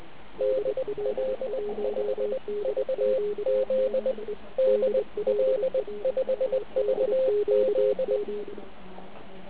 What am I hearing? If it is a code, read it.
Morse code, heard as a keyed tone, about 20 wpm, spaced as for 18 wpm, 520 Hz, with radio noise. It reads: DL5VZ DL5VZ